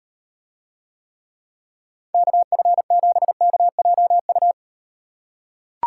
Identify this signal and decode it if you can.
Morse code, heard as a keyed tone, about 38 wpm, 700 Hz. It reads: KF7KJU